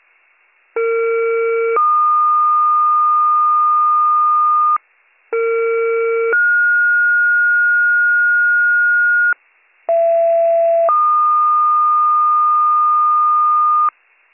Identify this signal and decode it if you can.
Two-tone page, heard as a one-second tone, then a three-second tone, about 460 then 1220 Hz, about 460 then 1430 Hz, about 670 then 1220 Hz. It is heard from 0.8 to 13.9 s.